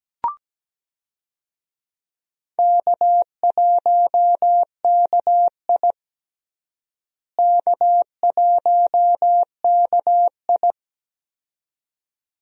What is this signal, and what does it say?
Morse code, heard as a keyed tone, about 17 wpm, 700 Hz. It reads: K1KI K1KI
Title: USA Call Signs - 4 Characters - Repeated 2x - 17wpm